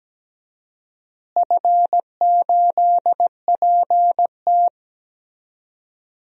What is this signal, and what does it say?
Morse code, heard as a keyed tone, about 17 wpm, 700 Hz. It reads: F8PT
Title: DXCC-100 Call Signs - 4 Characters - Repeated 3x - 17wpm